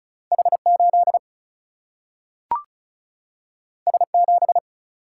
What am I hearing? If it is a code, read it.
Morse code, heard as a keyed tone, about 36 wpm, 700 Hz. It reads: H 8 S 7
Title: Single-Letter Single-Number 35wpm